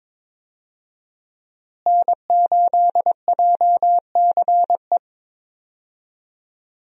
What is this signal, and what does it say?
Morse code, heard as a keyed tone, about 21 wpm, 690 Hz. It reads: N8JCE
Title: USA Call Signs - 5 Characters - Repeated 3x - 22wpm